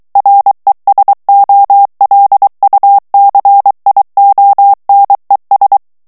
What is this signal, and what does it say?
Morse code, heard as a keyed tone, about 23 wpm, 800 Hz. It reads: RESOLUCIONES
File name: morse mp3 / ESPAGNOL TEXTE EN CLAIR 2 VITESSE 22 MOTS MINUTES